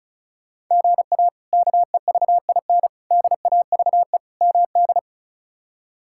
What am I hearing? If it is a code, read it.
Morse code, heard as a keyed tone, about 36 wpm, 700 Hz. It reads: GA KEVIN DAVE MD
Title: K1USN SST QSO Practice - 35wpm